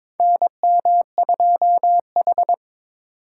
Morse code, heard as a keyed tone, about 22 wpm, 700 Hz. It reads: NM2H
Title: USA Call Signs - 4 Characters - 4x Character Spacing - 22wpm